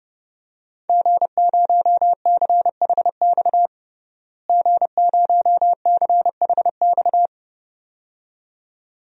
Morse code, heard as a keyed tone, about 30 wpm, 700 Hz. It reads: G0CHX G0CHX
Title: DXCC-100 Call Signs - 5 Characters - Repeated 2x - 30wpm